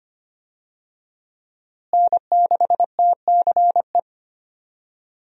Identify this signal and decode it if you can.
Morse code, heard as a keyed tone, about 25 wpm, 700 Hz. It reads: N6TCE
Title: USA Call Signs - 5 Characters - Repeated 3x - 25wpm